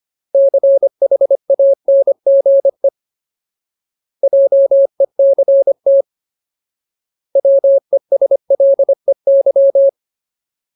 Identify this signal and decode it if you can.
Morse code, heard as a keyed tone, about 25 wpm, 550 Hz. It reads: CHANGE JECT WESLEY
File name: Ultimate Word List / 25WPM Ultimate